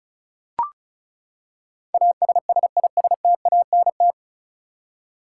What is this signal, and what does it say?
Morse code, heard as a keyed tone, about 36 wpm, 700 Hz. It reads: ASSISTANT
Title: Practice QSO - Exchange 3 - More Info - 35wpm